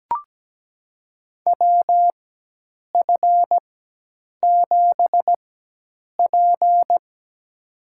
Morse code, heard as a keyed tone, about 17 wpm, spaced as for 8 wpm, 700 Hz. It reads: WF7P
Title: USA Call Signs - 4 Characters - 4x Character Spacing - 17wpm